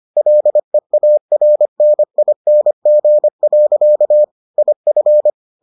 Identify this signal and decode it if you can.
Morse code, heard as a keyed tone, about 25 wpm, 600 Hz. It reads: LEARNING. IF